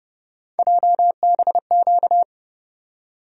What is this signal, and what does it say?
Morse code, heard as a keyed tone, about 30 wpm, 700 Hz. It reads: JBQ
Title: USA Call Sign Postfixes - 3 Letters 30wpm